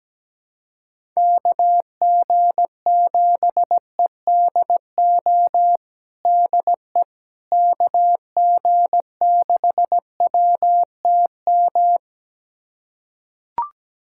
Morse code, heard as a keyed tone, about 17 wpm, 700 Hz. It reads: KG7EDO DE KG6WTM
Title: USA Callsign DE Callsign 17wpm